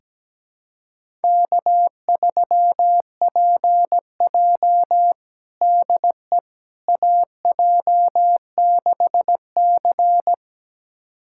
Morse code, heard as a keyed tone, about 17 wpm, 700 Hz. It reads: K3PJ DE AJ6C
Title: USA Callsign DE Callsign - 4 Characters - 17wpm